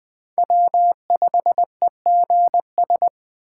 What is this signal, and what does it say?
Morse code, heard as a keyed tone, about 20 wpm, 700 Hz. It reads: W5EGS